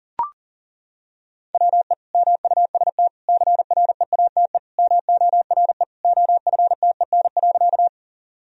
Morse code, heard as a keyed tone, about 40 wpm, 700 Hz. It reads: WE MUST CREATE MORE OFTEN.
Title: Sentences from Top 1000 Words - Review Part 3 - 40wpm